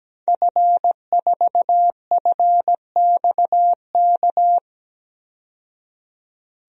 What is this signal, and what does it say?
Morse code, heard as a keyed tone, about 17 wpm, 700 Hz. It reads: F4FXK
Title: DXCC-100 Call Signs - 5 Characters - 17wpm